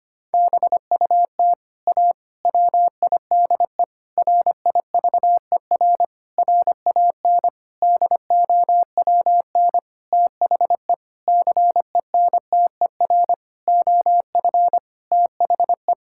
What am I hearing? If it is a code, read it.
Morse code, heard as a keyed tone, about 25 wpm, 700 Hz. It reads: BUT A WIDE RIVER RAN DOWN THE CENTER OF THE
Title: Sentences from Top 1000 Words - Review Part 6 - 25wpm